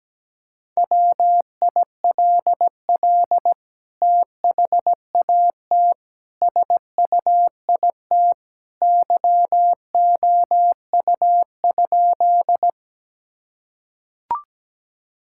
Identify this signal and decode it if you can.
Morse code, heard as a keyed tone, about 17 wpm, 700 Hz. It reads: WILL THAT SUIT YOU?
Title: Sentences from Top 1600 Words 17wpm